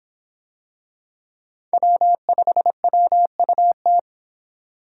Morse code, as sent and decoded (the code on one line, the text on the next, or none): .-- ..... .-- ..- -
W5WUT